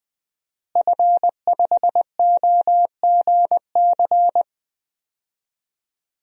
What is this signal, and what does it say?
Morse code, heard as a keyed tone, about 21 wpm, 700 Hz. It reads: F5OGC
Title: DXCC-100 Call Signs - 5 Characters - Repeated 2x - 20wpm